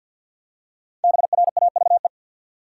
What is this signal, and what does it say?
Morse code, heard as a keyed tone, about 46 wpm, 700 Hz. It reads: BRAVE
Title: Five Letter Words - 50wpm